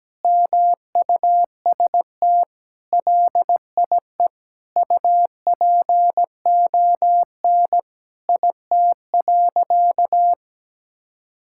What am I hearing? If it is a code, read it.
Morse code, heard as a keyed tone, about 17 wpm, 700 Hz. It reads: MUST LIE UPON IT.
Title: Sentences from Top 1600 Words 17wpm